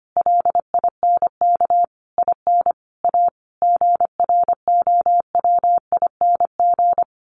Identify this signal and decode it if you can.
Morse code, heard as a keyed tone, about 25 wpm, 700 Hz. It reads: LINK IN A GROWING
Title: A Sound Of Thunder 25wpm